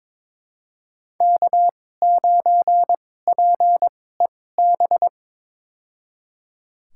Morse code, heard as a keyed tone, about 22 wpm, spaced as for 16 wpm, 700 Hz. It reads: K9PEB
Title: USA Call Signs - 5 Characters - 2x Character Spacing - 22wpm